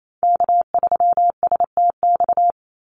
Morse code, heard as a keyed tone, about 28 wpm, 700 Hz. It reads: K3STX
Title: Call Signs Repeated 3x - 28wpm